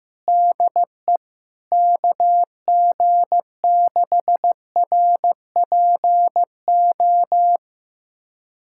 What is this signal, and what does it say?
Morse code, heard as a keyed tone, about 15 wpm, 700 Hz. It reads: DE KG6RPO
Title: USA Callsign DE Callsign 15wpm